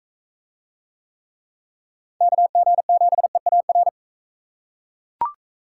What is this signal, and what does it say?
Morse code, heard as a keyed tone, about 39 wpm, 700 Hz. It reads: KG7EAR